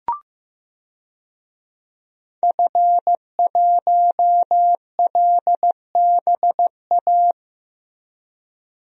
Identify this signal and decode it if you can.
Morse code, heard as a keyed tone, about 15 wpm, 700 Hz. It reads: F1LBA
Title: DXCC-100 Call Signs - 5 Characters - Repeated 2x - 15wpm